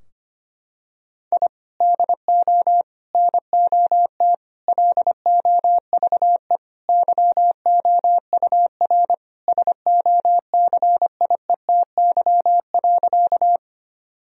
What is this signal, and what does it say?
Morse code, heard as a keyed tone, about 25 wpm, 700 Hz. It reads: I DO NOT LOVE YOUR SOCIETY.
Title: Sentences from Top 1000 Words - Review Part 4 - 25wpm